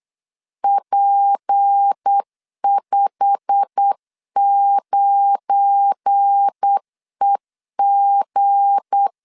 Morse code, heard as a keyed tone, about 8 wpm, 800 Hz. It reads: P59EG
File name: CW / DET / 1 9 1500C S C 8wpm 5653